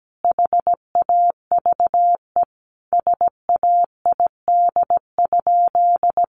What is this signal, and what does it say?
Morse code, heard as a keyed tone, about 17 wpm, 700 Hz. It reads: HAVE SAID?
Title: Sentences from Top 1500 Words 17wpm